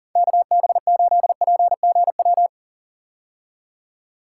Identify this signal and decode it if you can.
Morse code, heard as a keyed tone, about 40 wpm, 700 Hz. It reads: KB8PGW
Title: K1USN SST QSO Practice - 40wpm